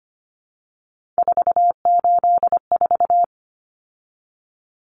Morse code, heard as a keyed tone, about 25 wpm, 700 Hz. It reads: 484
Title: Numbers - 3-Digit 25wpm